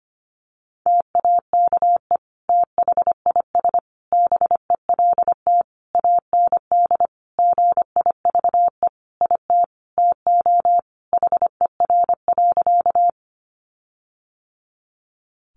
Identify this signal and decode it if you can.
Morse code, heard as a keyed tone, about 25 wpm, 700 Hz. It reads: TAKE THIS BELT AND GIVE IT TO HER.
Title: Sentences from Top 1400 Words 25wpm